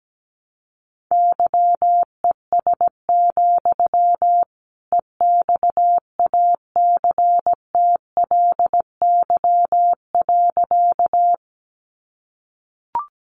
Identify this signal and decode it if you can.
Morse code, heard as a keyed tone, about 17 wpm, 700 Hz. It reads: YES, EXACTLY.
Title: Sentences from Top 900 Words 17wpm